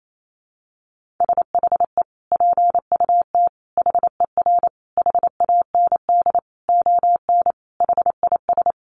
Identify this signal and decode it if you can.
Morse code, heard as a keyed tone, about 28 wpm, 700 Hz. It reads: SHE PUT HER HAND ON HIS
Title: Sentences from Top 1500 Words 28wpm